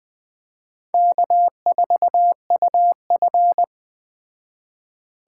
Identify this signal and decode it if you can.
Morse code, heard as a keyed tone, about 20 wpm, 700 Hz. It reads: K4UF
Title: Speed Racing - USA 1x2 Call Signs 17wpm